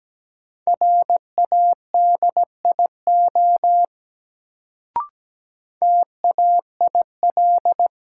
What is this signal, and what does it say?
Morse code, heard as a keyed tone, about 17 wpm, 700 Hz. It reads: RADIO TAIL
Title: Sentences from Top 1000 Words - Review Part 3 - New Word Drill - 17wpm